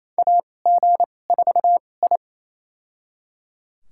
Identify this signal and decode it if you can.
Morse code, heard as a keyed tone, about 28 wpm, spaced as for 20 wpm, 700 Hz. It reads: AG4I
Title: USA Call Signs - 4 Characters - 2x Character Spacing - 28wpm